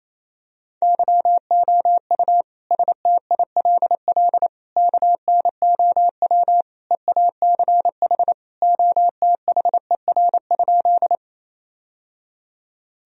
Morse code, heard as a keyed tone, about 28 wpm, 700 Hz. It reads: YOU STILL KNOW EACH OTHER?
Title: Sentences from Top 1000 Words - Review Part 1 - 28wpm